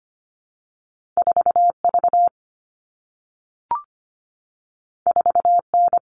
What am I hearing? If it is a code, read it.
Morse code, heard as a keyed tone, about 25 wpm, 700 Hz. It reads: 4V 4N
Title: All Cut Numbers - 2-Digit 25wpm